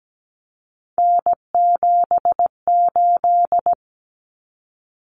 Morse code, heard as a keyed tone, about 17 wpm, 700 Hz. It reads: N78
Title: Common Cut Numbers - 3-Digit 17wpm